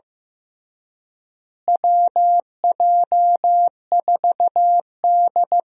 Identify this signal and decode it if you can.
Morse code, heard as a keyed tone, about 14 wpm, 700 Hz. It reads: WJ4D
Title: USA Call Signs - 4 Characters - Repeated 3x - 15wpm